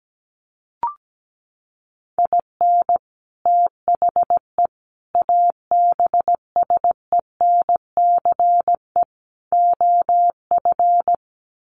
Morse code, heard as a keyed tone, about 17 wpm, 700 Hz. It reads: IN THE ABSENCE OF
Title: Sets of 4 Words 17wpm